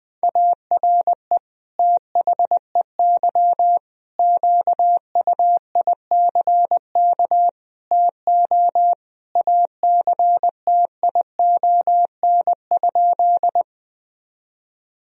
Morse code, heard as a keyed tone, about 20 wpm, 700 Hz. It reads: ARE THEY QUICK TO ACTION?
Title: Sentences from Top 1000 Words - Review Part 4 - 20wpm